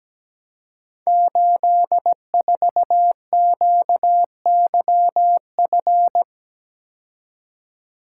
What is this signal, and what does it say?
Morse code, heard as a keyed tone, about 17 wpm, 700 Hz. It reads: 84QYF
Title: Code Groups - 5 Letter:Numbers - 17wpm